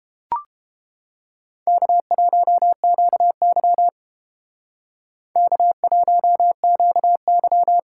Morse code, heard as a keyed tone, about 29 wpm, 700 Hz. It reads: K1QY K1QY